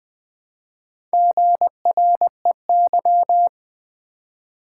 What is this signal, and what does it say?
Morse code, heard as a keyed tone, about 20 wpm, 700 Hz. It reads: GREY
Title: Build A Word - 2 Letters At A Time - 4-Letter Words - 20wpm